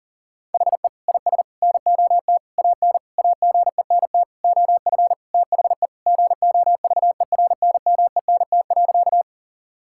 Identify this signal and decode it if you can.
Morse code, heard as a keyed tone, about 37 wpm, 700 Hz. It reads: HE IS NOT AN AGENT OF THE GOVERNMENT.
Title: Sentences from Top 1600 Words 40wpm